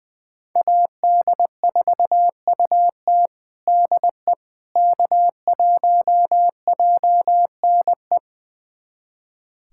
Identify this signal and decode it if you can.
Morse code, heard as a keyed tone, about 21 wpm, 700 Hz. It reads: AD4UT DE K1JNE
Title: USA Callsign DE Callsign - 5 Characters - 20wpm